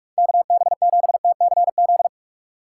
Morse code, heard as a keyed tone, about 49 wpm, 700 Hz. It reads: KB7TCZ